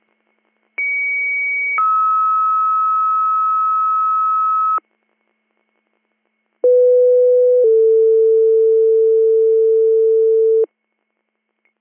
Two-tone page, heard as a one-second tone, then a three-second tone, about 2250 then 1290 Hz, about 500 then 450 Hz. It is heard from 0.8 to 10.6 s.